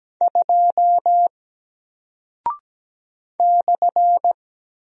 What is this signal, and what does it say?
Morse code, heard as a keyed tone, about 17 wpm, 700 Hz. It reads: 2 /
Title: Single Character (Letter:Number:Punctuation:Prosign) 17wpm